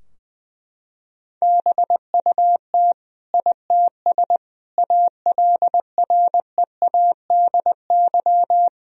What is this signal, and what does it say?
Morse code, heard as a keyed tone, about 21 wpm, 700 Hz. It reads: BUT ITS ALREADY
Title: Sentences from Top 1000 Words - Review Part 2 - 20wpm